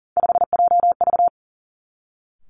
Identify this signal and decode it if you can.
Morse code, heard as a keyed tone, about 39 wpm, 700 Hz. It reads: 5JV
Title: Three Challenging Letter-Number Combinations 40wpm